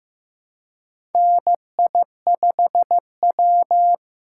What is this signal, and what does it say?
Morse code, heard as a keyed tone, about 15 wpm, 700 Hz. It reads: NI5W
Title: USA Call Signs - 4 Characters - 3x Character Spacing - 15wpm